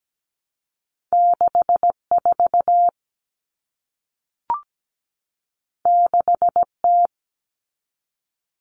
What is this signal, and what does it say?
Morse code, heard as a keyed tone, about 17 wpm, 700 Hz. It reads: 64 6T
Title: Common Cut Numbers - 2-Digit 17wpm